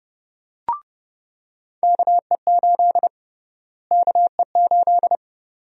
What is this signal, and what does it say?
Morse code, heard as a keyed tone, about 31 wpm, 700 Hz. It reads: KE8 KE8
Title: USA Call Sign Prefixes - Repeated 2x - 30wpm